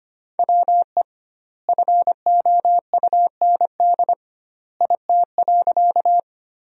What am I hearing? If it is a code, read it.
Morse code, heard as a keyed tone, about 25 wpm, 700 Hz. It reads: WE FOUND IT.
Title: 2x Word-Spacing - Sentences from Top 300 Words - 25wpm